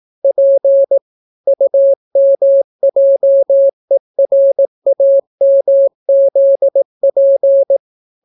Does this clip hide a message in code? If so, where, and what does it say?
Yes: Morse code from 0.2 to 7.8 s, heard as a keyed tone, about 18 wpm, 550 Hz. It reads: P UMJERAMZP